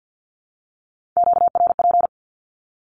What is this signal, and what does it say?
Morse code, heard as a keyed tone, about 50 wpm, 700 Hz. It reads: QRP
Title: Q-Codes Repeated 3x 50wpm